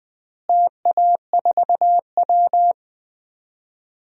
Morse code, heard as a keyed tone, about 21 wpm, 700 Hz. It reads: TA4W
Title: DXCC-100 Call Signs - 4 Characters - Repeated 3x - 20wpm